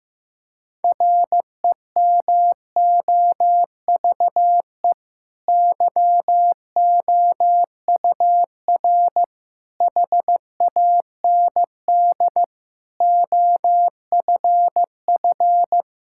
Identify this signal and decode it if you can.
Morse code, heard as a keyed tone, about 16 wpm, 700 Hz. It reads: REMOVE YOUR HAND OFF
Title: Sentences from Top 1000 Words - Review Part 5 - 15wpm